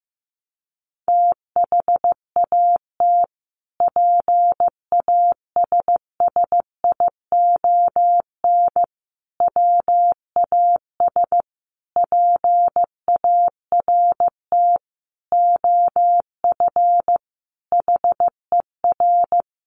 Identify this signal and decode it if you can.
Morse code, heard as a keyed tone, about 15 wpm, 700 Hz. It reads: THAT PASSION WAS PART OF HER
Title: Sentences from Top 1300 Words 15wpm